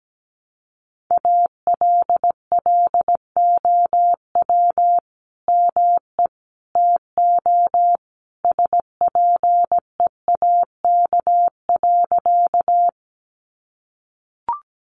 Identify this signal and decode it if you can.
Morse code, heard as a keyed tone, about 17 wpm, 700 Hz. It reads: ALLOW ME TO SPEAK.
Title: Sentences from Top 1100 Words 17wpm